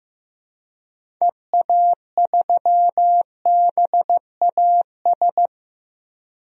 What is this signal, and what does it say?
Morse code, heard as a keyed tone, about 15 wpm, 700 Hz. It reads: EA3BAS
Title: DXCC-100 Call Signs - 6 Characters - 15wpm